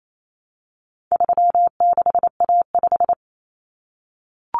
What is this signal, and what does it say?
Morse code, heard as a keyed tone, about 28 wpm, 700 Hz. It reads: 36A5
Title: Common Cut Numbers - 4-Digit 28wpm